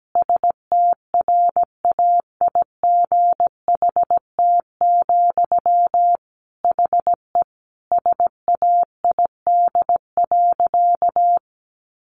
Morse code, heard as a keyed tone, about 17 wpm, 700 Hz. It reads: STRAIGHT, HE SAID.
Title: Sentences from Top 1000 Words 17wpm